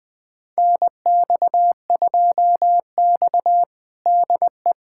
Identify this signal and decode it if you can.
Morse code, heard as a keyed tone, about 21 wpm, 700 Hz. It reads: NX2X DE
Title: USA Callsign DE Callsign - 4 Characters - 20wpm